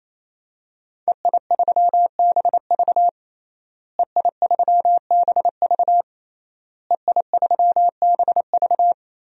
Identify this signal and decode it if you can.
Morse code, heard as a keyed tone, about 27 wpm, 700 Hz. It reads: EI3BV EI3BV EI3BV
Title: DXCC-100 Call Signs - 5 Characters - Repeated 3x - 28wpm